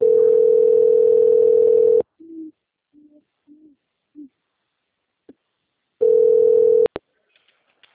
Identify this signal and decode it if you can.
Ringback tone (North America), 440 + 480 Hz.